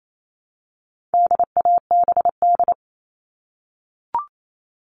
Morse code, heard as a keyed tone, about 28 wpm, 700 Hz. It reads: DABD
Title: All Cut Numbers - 4-Digit 28wpm